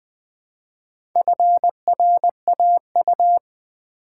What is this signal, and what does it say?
Morse code, heard as a keyed tone, about 20 wpm, 700 Hz. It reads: FRAU
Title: Build A Word - 1 Letter At A Time - 5-Letter Words - 20wpm